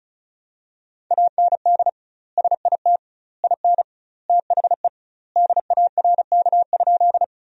Morse code, heard as a keyed tone, about 36 wpm, 700 Hz. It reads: AND SIT IN THE DARK?
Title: 2x Word-Spacing - Sentences from Top 500 Words - 35wpm